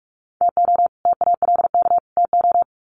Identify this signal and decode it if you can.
Morse code, heard as a keyed tone, about 44 wpm, 700 Hz. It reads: TO TALK TO
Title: Sets of 3 Words 45wpm